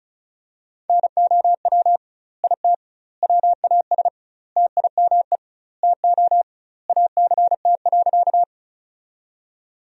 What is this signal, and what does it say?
Morse code, heard as a keyed tone, about 34 wpm, 700 Hz. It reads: NOW IT WAS TIME TO ACT.
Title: 2x Word-Spacing - Sentences from Top 200 Words - 35wpm